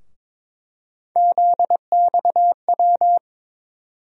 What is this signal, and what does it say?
Morse code, heard as a keyed tone, about 21 wpm, 700 Hz. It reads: ZXW
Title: USA Call Sign Postfixes - 3 Letters 22wpm